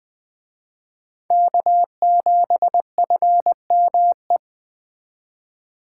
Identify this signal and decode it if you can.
Morse code, heard as a keyed tone, about 20 wpm, 700 Hz. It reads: K7FME
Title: Speed Racing - USA Call Signs 17wpm